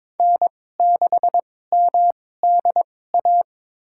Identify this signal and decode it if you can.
Morse code, heard as a keyed tone, about 22 wpm, spaced as for 16 wpm, 700 Hz. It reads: N6MDA
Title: USA Call Signs - 2x Character Spacing - 22wpm